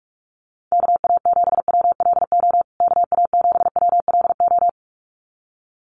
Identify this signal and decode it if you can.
Morse code, heard as a keyed tone, about 44 wpm, 700 Hz. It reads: KA7WLO KA7WLO
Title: USA Call Signs Repeated 2x 45wpm